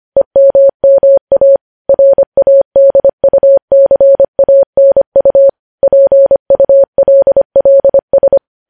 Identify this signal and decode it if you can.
Morse code, heard as a keyed tone, about 25 wpm, 550 Hz. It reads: EMMA RADUCANU PULLS
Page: news-headlines-in-morse-code-at-25-wpm